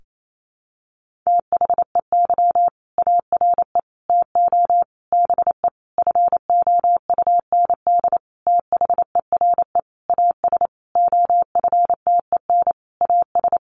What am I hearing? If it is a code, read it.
Morse code, heard as a keyed tone, about 28 wpm, 700 Hz. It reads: THEY ARE TO BE FOUND THERE AS OFTEN AS
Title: Sentences from Top 1400 Words 28wpm